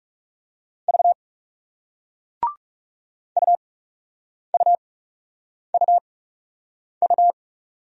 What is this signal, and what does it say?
Morse code, heard as a keyed tone, about 41 wpm, spaced as for 8 wpm, 700 Hz. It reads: V UUUU